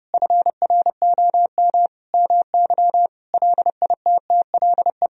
Morse code, heard as a keyed tone, about 30 wpm, 700 Hz. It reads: FROM MY LITTLE
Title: Sentences from Top 1000 Words - Review Part 4 - 30wpm